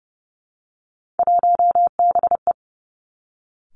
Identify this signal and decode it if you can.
Morse code, heard as a keyed tone, about 30 wpm, 700 Hz. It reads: 1BE
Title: Three Challenging Letter-Number Combinations 30wpm